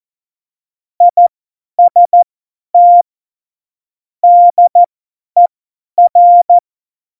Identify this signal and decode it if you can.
Morse code, heard as a keyed tone, about 14 wpm, spaced as for 10 wpm, 700 Hz. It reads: IST DER
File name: Morsekurs / Witze / Witz 26 Mai 2020, 50 F70